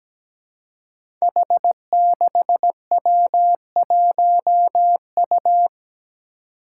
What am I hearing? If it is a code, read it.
Morse code, heard as a keyed tone, about 17 wpm, 700 Hz. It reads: H6W1U
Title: Five Challenging Letter-Number Combinations 17wpm